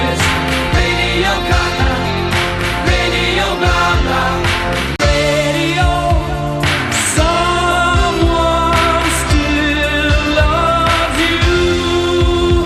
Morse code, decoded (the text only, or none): none